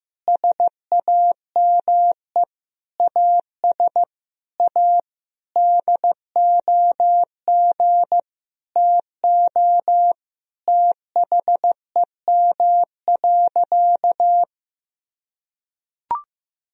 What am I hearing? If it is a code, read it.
Morse code, heard as a keyed tone, about 15 wpm, 700 Hz. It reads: SAME AS A DOG TO THEM.
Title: Sentences from Top 1000 Words - Review Part 2 - 15wpm